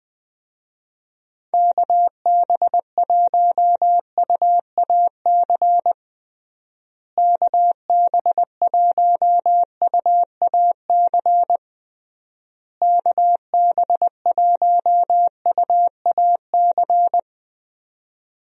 Morse code, heard as a keyed tone, about 20 wpm, 700 Hz. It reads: KB1UAC KB1UAC KB1UAC
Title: USA Call Signs - 6 Characters - Repeated 3x - 20wpm